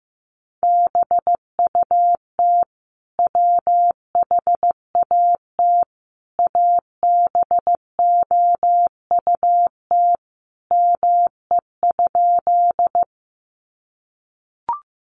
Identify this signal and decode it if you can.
Morse code, heard as a keyed tone, about 15 wpm, 700 Hz. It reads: BUT WHAT ABOUT ME?
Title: Sentences from Top 200 Words 15wpm